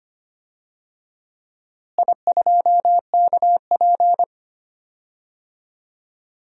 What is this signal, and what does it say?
Morse code, heard as a keyed tone, about 25 wpm, 700 Hz. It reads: I2KP
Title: DXCC-100 Call Signs - 4 Characters - Repeated 3x - 25wpm